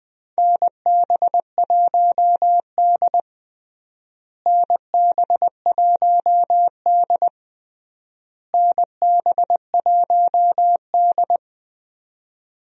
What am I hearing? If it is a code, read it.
Morse code, heard as a keyed tone, about 20 wpm, 700 Hz. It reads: NB1D NB1D NB1D
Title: USA Call Signs - 4 Characters - Repeated 3x - 20wpm